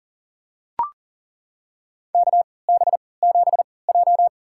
Morse code, heard as a keyed tone, about 40 wpm, spaced as for 23 wpm, 700 Hz. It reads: KB7J
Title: USA Call Signs - 4 Characters - 3x Character Spacing - 40wpm